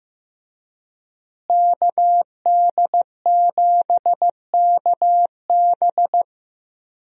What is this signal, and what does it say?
Morse code, heard as a keyed tone, about 15 wpm, 700 Hz. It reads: KD7KB
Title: USA Call Signs - 5 Characters - Repeated 2x - 15wpm